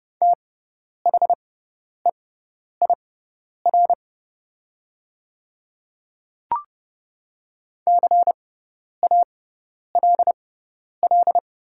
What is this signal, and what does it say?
Morse code, heard as a keyed tone, about 30 wpm, spaced as for 10 wpm, 700 Hz. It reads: THEIR CALL
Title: 6x Character Spacing - Top 100 Words - 30wpm